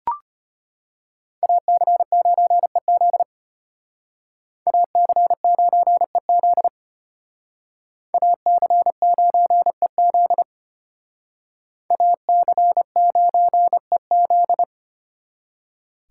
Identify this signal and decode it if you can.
Morse code, heard as a keyed tone, about 31 wpm, 700 Hz. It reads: AC9EZ AC9EZ AC9EZ AC9EZ
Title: Speed Racing - USA Call Signs 25wpm